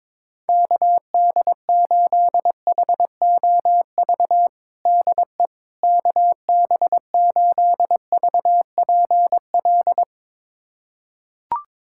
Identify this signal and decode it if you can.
Morse code, heard as a keyed tone, about 22 wpm, 700 Hz. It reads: KD8HOV DE KB8VPL
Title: USA Callsign DE Callsign - 6 Characters - 22wpm